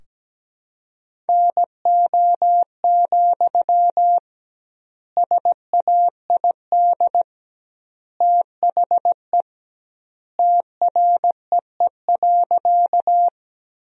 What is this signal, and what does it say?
Morse code, heard as a keyed tone, about 17 wpm, 700 Hz. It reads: NO, SAID THE TREE.
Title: 2x Word-Spacing - Sentences from Top 300 Words - 17wpm